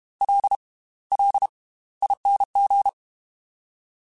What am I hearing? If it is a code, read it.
Morse code, heard as a keyed tone, about 32 wpm, spaced as for 18 wpm, 800 Hz. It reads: LLING